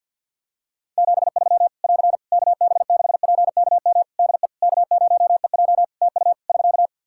Morse code, heard as a keyed tone, about 46 wpm, 700 Hz. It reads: 73 <AR> KB6PKM DE K0EJ TU <SK>
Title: Practice QSO - Exchange 4 - Ending - 50wpm